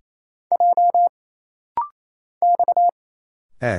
Morse code, heard as a keyed tone, about 27 wpm, 700 Hz. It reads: J X E
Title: Single Letters - Rapid-Fire 28wpm